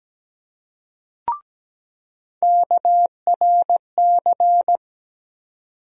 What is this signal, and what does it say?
Morse code, heard as a keyed tone, about 17 wpm, 700 Hz. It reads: KRC
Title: USA Call Sign Postfixes - 3 Letters 17wpm